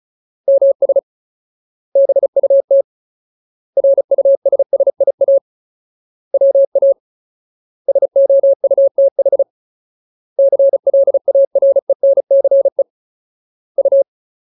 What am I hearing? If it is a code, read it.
Morse code, heard as a keyed tone, about 34 wpm, 550 Hz. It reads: MS BUT RUSSIA WA SOUTH CLARENCE U
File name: Ultimate Word List / 35WPM Ultimate